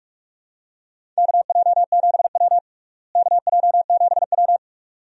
Morse code, heard as a keyed tone, about 47 wpm, 700 Hz. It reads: KJ7W KJ7W
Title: USA Call Signs - 4 Characters - Repeated 2x - 45wpm